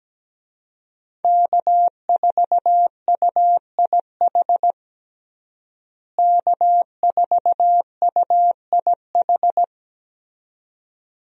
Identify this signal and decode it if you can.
Morse code, heard as a keyed tone, about 17 wpm, 700 Hz. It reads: K4UIH K4UIH
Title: USA Call Signs - 5 Characters - Repeated 2x - 17wpm